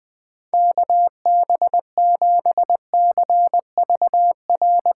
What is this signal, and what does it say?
Morse code, heard as a keyed tone, about 20 wpm, 700 Hz. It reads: KB7CVR
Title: USA Call Signs - 6 Characters - 3x Character Spacing - 20wpm